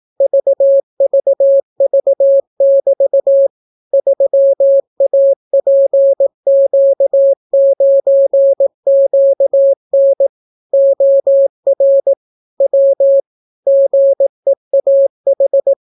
Morse code, heard as a keyed tone, about 18 wpm, 550 Hz. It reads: VVV<BT> 3APQ9QN OR W GEAH